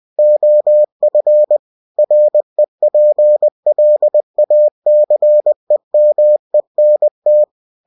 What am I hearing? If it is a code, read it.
Morse code, heard as a keyed tone, about 20 wpm, 600 Hz. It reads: OF REPLACEMENT
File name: cw-books / winnie-the-pooh / winnie-20wpm-page-0119